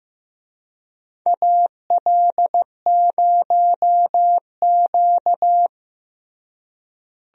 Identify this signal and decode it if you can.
Morse code, heard as a keyed tone, about 15 wpm, 700 Hz. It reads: AL0Q
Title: USA Call Signs - 4 Characters - Repeated 3x - 15wpm